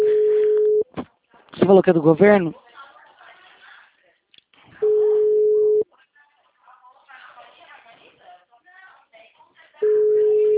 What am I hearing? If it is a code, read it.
Ringback tone (Europe, 425 Hz), with speech.